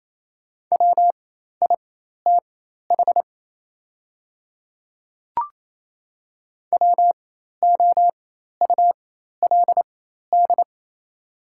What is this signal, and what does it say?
Morse code, heard as a keyed tone, about 27 wpm, spaced as for 13 wpm, 700 Hz. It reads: WITH WOULD
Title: 4x Character Spacing - Top 500 Words - 28wpm